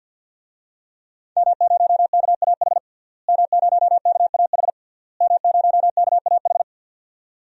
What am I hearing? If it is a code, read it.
Morse code, heard as a keyed tone, about 51 wpm, 700 Hz. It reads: M0KAH M0KAH M0KAH
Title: DXCC-100 Call Signs - 5 Characters - Repeated 3x - 50wpm